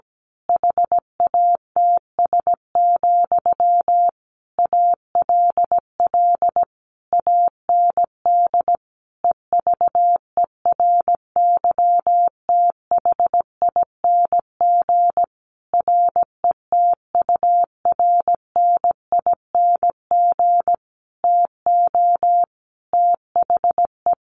Morse code, heard as a keyed tone, about 17 wpm, 700 Hz. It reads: HATS, ALL AND EVERYTHING RETURNING TO THE
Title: A Sound Of Thunder 17wpm